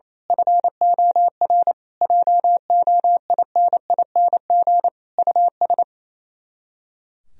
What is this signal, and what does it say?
Morse code, heard as a keyed tone, about 27 wpm, 700 Hz. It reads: FOR JOINING US
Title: Speed-Racing - Sets of 5 Words 28wpm